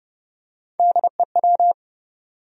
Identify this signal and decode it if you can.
Morse code, heard as a keyed tone, about 30 wpm, 700 Hz. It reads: DEW
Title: USA Call Sign Postfixes - 3 Letters 30wpm